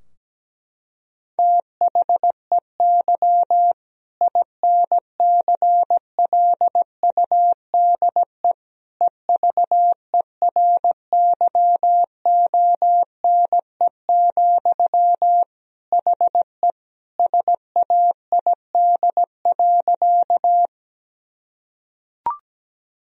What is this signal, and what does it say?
Morse code, heard as a keyed tone, about 17 wpm, 700 Hz. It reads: THEY INCLUDE EVERYONE, HE SAID.
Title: Sentences from Top 1000 Words - Review Part 3 - 17wpm